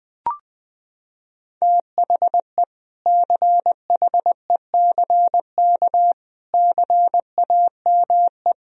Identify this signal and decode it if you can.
Morse code, heard as a keyed tone, about 21 wpm, 700 Hz. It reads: THE CHECK CAME
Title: Sentences from Top 1000 Words - Review Part 3 - 20wpm